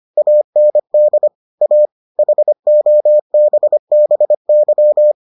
Morse code, heard as a keyed tone, about 25 wpm, 600 Hz. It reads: AND A HOBBY